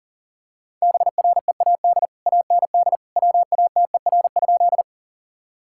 Morse code, heard as a keyed tone, about 40 wpm, 700 Hz. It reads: BREAD AND WATER?
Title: Sentences from Top 1600 Words 40wpm